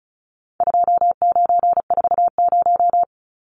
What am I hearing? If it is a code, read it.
Morse code, heard as a keyed tone, about 34 wpm, 700 Hz. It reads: 2940
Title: Numbers - 4-Digit 35wpm